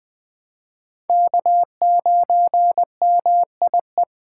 Morse code, heard as a keyed tone, about 20 wpm, 700 Hz. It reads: K9MIE
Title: USA Call Signs - 4x Character Spacing - 20wpm